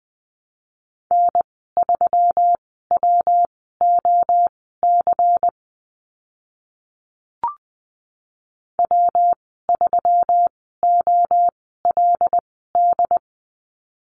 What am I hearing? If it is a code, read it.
Morse code, heard as a keyed tone, about 19 wpm, spaced as for 14 wpm, 700 Hz. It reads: N3WOC W3OLD
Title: 140 Lesson 23g - Letter f - Call Signs 20wpm